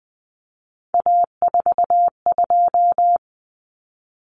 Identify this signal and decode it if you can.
Morse code, heard as a keyed tone, about 20 wpm, 700 Hz. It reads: A42
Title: Common Cut Numbers - 3-Digit 20wpm